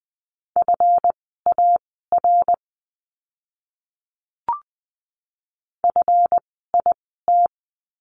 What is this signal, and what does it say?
Morse code, heard as a keyed tone, about 20 wpm, spaced as for 14 wpm, 700 Hz. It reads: FAR FIT
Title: 139 Lesson 23f - Letter f - Words 20wpm